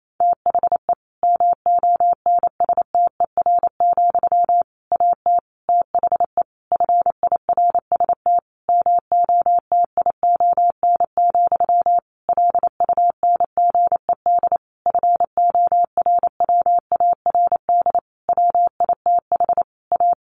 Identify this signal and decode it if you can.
Morse code, heard as a keyed tone, about 28 wpm, 700 Hz. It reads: THE MONSTER, AT THE FIRST MOTION, LUNGED FORWARD WITH A
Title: A Sound Of Thunder 28wpm